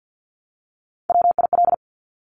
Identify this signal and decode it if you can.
Morse code, heard as a keyed tone, about 50 wpm, 700 Hz. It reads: WIL
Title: Top Three Letter Combinations in Words 50wpm